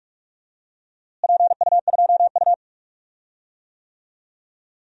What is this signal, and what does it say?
Morse code, heard as a keyed tone, about 41 wpm, 700 Hz. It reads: PU2U
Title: DXCC-100 Call Signs - 4 Characters - Repeated 3x - 45wpm